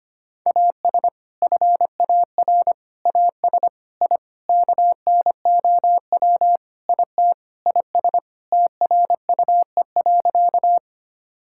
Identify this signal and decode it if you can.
Morse code, heard as a keyed tone, about 25 wpm, 700 Hz. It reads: AS FAR AS I KNOW IT IS TRUE.
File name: practice / 25wpm / Sentences with 4-Word Phrases - 25wpm